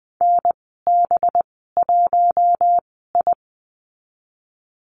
Morse code, heard as a keyed tone, about 20 wpm, spaced as for 14 wpm, 700 Hz. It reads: NB1I
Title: 193 Lesson 31g - Letter v - Call Signs 20wpm